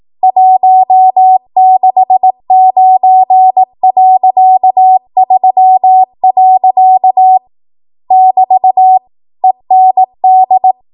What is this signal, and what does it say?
Morse code, heard as a keyed tone, about 18 wpm, 750 Hz. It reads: 169.3. <BT> END
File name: Morse / 18WPM